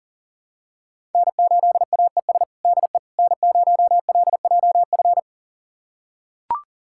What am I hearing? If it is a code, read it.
Morse code, heard as a keyed tone, about 40 wpm, 700 Hz. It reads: N8AES DE N0LJF
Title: USA Callsign DE Callsign - 5 Characters - 40wpm